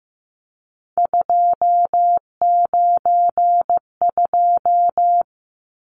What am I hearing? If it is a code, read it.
Morse code, heard as a keyed tone, about 15 wpm, 700 Hz. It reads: 292
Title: Numbers - 3-Digit 15wpm